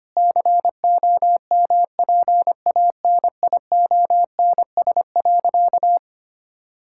Morse code, heard as a keyed tone, about 25 wpm, 700 Hz. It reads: COMPANIONS.
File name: practice / 25wpm / Difficult Sentences with Hints - 25wpm